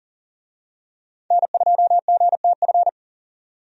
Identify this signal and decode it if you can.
Morse code, heard as a keyed tone, about 40 wpm, 700 Hz. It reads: N2GTF
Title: USA Call Signs - 5 Characters - 40wpm